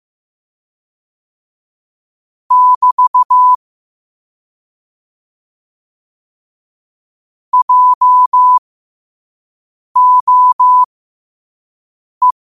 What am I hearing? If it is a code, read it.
Morse code, heard as a keyed tone, about 15 wpm, spaced as for 5 wpm, 1000 Hz. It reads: <BT> JOE